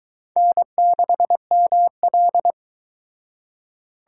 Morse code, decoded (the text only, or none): N6ML